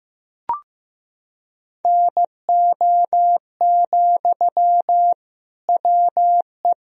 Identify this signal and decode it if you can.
Morse code, heard as a keyed tone, about 15 wpm, 700 Hz. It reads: NO, WE
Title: Sentences from Top 1000 Words - Review Part 3 - 15wpm